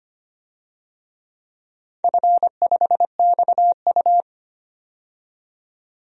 Morse code, heard as a keyed tone, about 25 wpm, 700 Hz. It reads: F5XU
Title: DXCC-100 Call Signs - 4 Characters - 25wpm